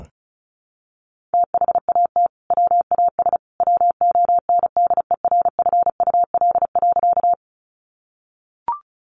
Morse code, heard as a keyed tone, about 34 wpm, 700 Hz. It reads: THAT WAS WONDERFUL.
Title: Sentences from Top 1100 Words 35wpm